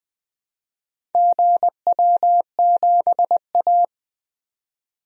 Morse code, heard as a keyed tone, about 20 wpm, 700 Hz. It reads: GW7A
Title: DXCC-100 Call Signs - 4 Characters - 20wpm